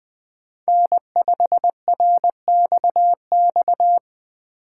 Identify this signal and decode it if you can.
Morse code, heard as a keyed tone, about 20 wpm, 700 Hz. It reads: N5RXX
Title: USA Call Signs - 5 Characters - 6x Character Spacing - 20wpm